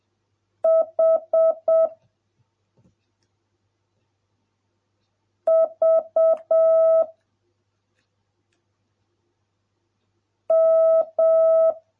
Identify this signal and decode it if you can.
Morse code, heard as a keyed tone, about 7 wpm, 650 Hz. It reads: H V M